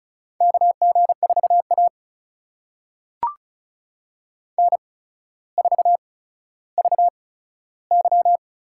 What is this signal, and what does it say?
Morse code, heard as a keyed tone, about 36 wpm, spaced as for 15 wpm, 700 Hz. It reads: KG4A N4VY
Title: USA Call Signs - 4 Characters - 8x Character Spacing - 35wpm